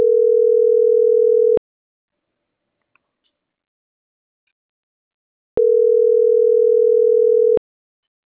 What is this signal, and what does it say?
Ringback tone (North America), 440 + 480 Hz.